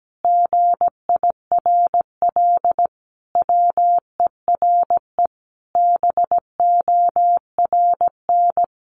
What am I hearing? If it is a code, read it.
Morse code, heard as a keyed tone, about 17 wpm, 700 Hz. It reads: GIRL WERE BORN
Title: Sentences from Top 1200 Words 17wpm